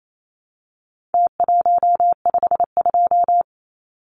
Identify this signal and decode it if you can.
Morse code, heard as a keyed tone, about 28 wpm, 700 Hz. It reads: T152
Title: Numbers - Contest Style - 4-Digit 28wpm